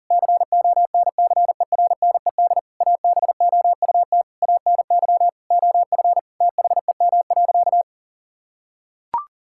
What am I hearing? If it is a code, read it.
Morse code, heard as a keyed tone, about 40 wpm, 700 Hz. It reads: CONCERNED ABOUT ANY OF THEM.
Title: Sentences from Top 1000 Words - Review Part 6 - 40wpm